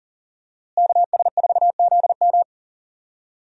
Morse code, heard as a keyed tone, about 40 wpm, 700 Hz. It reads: KS4ZM
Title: USA Call Signs - 5 Characters - Repeated 2x - 40wpm